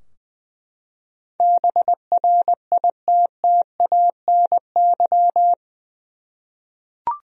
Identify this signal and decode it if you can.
Morse code, heard as a keyed tone, about 20 wpm, 700 Hz. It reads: BRITTANY
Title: USA Common Names - 20wpm